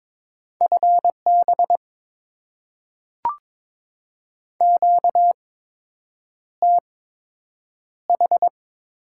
Morse code, heard as a keyed tone, about 22 wpm, spaced as for 8 wpm, 700 Hz. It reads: FB QTH
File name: Top 20 QSO Elements - 8x Character Spacing - 22wpm